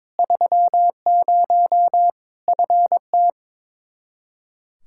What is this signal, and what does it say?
Morse code, heard as a keyed tone, about 22 wpm, 700 Hz. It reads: 30 FT
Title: Practice QSO - Exchange 2 - Info - 22wpm